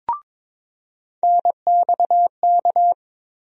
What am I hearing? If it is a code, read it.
Morse code, heard as a keyed tone, about 22 wpm, 700 Hz. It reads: NXK
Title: USA Call Sign Postfixes - 3 Letters 22wpm